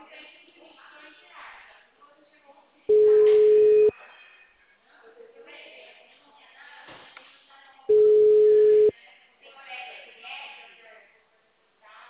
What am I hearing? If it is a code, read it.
Ringback tone (Europe, 425 Hz), with speech.